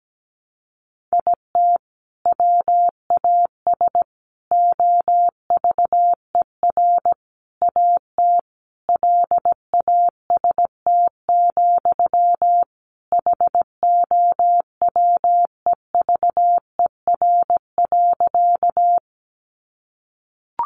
Morse code, heard as a keyed tone, about 17 wpm, 700 Hz. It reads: IT WAS OVER AT LAST, HOWEVER.
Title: Sentences from Top 600 Words 17wpm